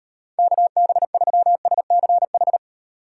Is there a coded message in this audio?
Yes: Morse code from 0.4 to 2.6 s, heard as a keyed tone, about 34 wpm, 700 Hz. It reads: KB3SCH